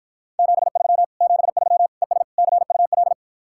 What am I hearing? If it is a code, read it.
Morse code, heard as a keyed tone, about 52 wpm, 700 Hz. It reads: 73 73 ES CUL